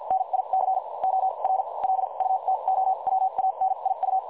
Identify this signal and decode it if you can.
Morse code, heard as a keyed tone, about 51 wpm, spaced as for 40 wpm, 760 Hz. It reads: NIC OR LUTOWANIU
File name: dlazaawansowanych / nagraniazpasma / websdr recording 3565.0kHz